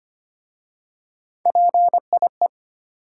Morse code, heard as a keyed tone, about 25 wpm, 700 Hz. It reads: PIE
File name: practice / 25wpm / Easy Words with Hints - 25wpm